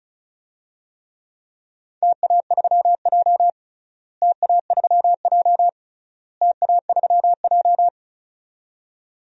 Morse code, heard as a keyed tone, about 36 wpm, 700 Hz. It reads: TA3J TA3J TA3J
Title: DXCC-100 Call Signs - 4 Characters - Repeated 3x - 35wpm